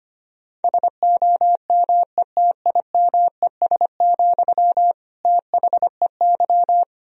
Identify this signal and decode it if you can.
Morse code, heard as a keyed tone, about 25 wpm, 700 Hz. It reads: SOMETIMES, THEY
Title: Sentences from Top 1000 Words - Review Part 4 - 25wpm